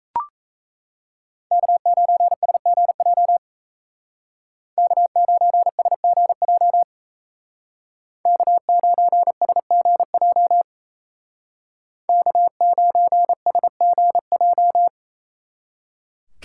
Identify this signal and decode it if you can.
Morse code, heard as a keyed tone, about 33 wpm, 700 Hz. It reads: K9SGJ K9SGJ K9SGJ K9SGJ
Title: Speed Racing - USA 1x3 Call Signs 28wpm